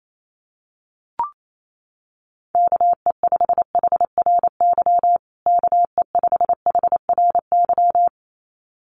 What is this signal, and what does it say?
Morse code, heard as a keyed tone, about 28 wpm, 700 Hz. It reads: KE5HRY KE5HRY
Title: USA Call Signs Repeated 2x 28wpm